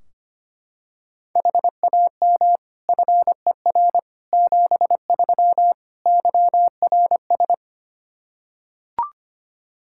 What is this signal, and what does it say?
Morse code, heard as a keyed tone, about 25 wpm, 700 Hz. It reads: HAM FER 73 YRS
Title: Practice QSO - Exchange 3 - More Info - 25wpm